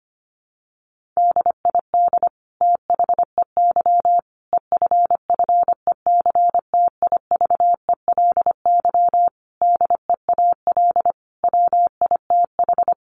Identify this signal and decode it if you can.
Morse code, heard as a keyed tone, about 25 wpm, 700 Hz. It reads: DID THEY EFFECTIVELY DEAL WITH
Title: Sentences from Top 1400 Words 25wpm